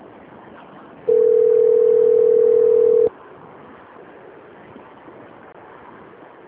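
Ringback tone (North America), 440 + 480 Hz.
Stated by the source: Outgoing call recorded on a phone system in Itatiba.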